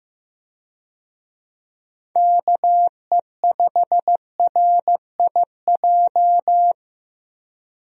Morse code, heard as a keyed tone, about 15 wpm, 700 Hz. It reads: KE5RIJ
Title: USA Call Signs - 6 Characters - 15wpm